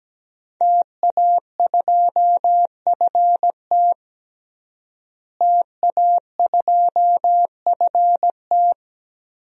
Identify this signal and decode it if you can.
Morse code, heard as a keyed tone, about 17 wpm, 700 Hz. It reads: TA2FT TA2FT
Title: DXCC-100 Call Signs - 5 Characters - Repeated 2x - 17wpm